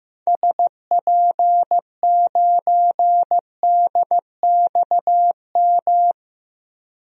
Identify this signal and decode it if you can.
Morse code, heard as a keyed tone, about 15 wpm, 700 Hz. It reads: SP9DXM
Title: DXCC-100 Call Signs - 6 Characters - 15wpm